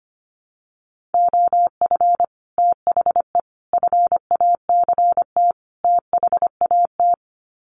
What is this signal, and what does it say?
Morse code, heard as a keyed tone, about 25 wpm, 700 Hz. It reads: OF THE FACT THAT
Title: Sets of 4 Words 25wpm